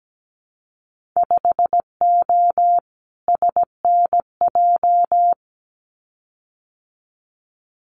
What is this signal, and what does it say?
Morse code, heard as a keyed tone, about 17 wpm, 700 Hz. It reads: 5O SNJ
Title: Winter Field Day Exchanges Encore 17wpm